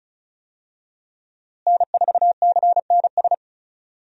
Morse code, heard as a keyed tone, about 36 wpm, 700 Hz. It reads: N4CNS
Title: USA Call Signs - 5 Characters - Repeated 2x - 35wpm